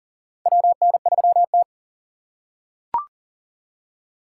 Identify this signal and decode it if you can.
Morse code, heard as a keyed tone, about 42 wpm, 690 Hz. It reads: WN3T
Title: USA Call Signs - 4 Characters - 6x Character Spacing - 40wpm